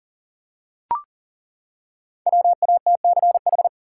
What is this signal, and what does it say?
Morse code, heard as a keyed tone, about 40 wpm, 700 Hz. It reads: WATCH
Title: Five Letter Words - 40wpm